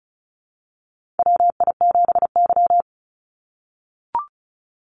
Morse code, heard as a keyed tone, about 35 wpm, 700 Hz. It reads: WI7Y
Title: USA Call Signs 35wpm